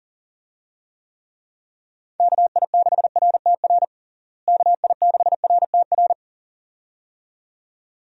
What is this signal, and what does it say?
Morse code, heard as a keyed tone, about 38 wpm, 700 Hz. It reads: KI6RTR KI6RTR
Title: USA Call Signs - 6 Characters - Repeated 2x - 40wpm